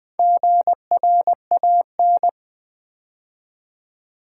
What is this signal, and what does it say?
Morse code, heard as a keyed tone, about 20 wpm, 700 Hz. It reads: GRAN
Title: Build A Word - 4 Letters At A Time - 11-Letter Words - 20wpm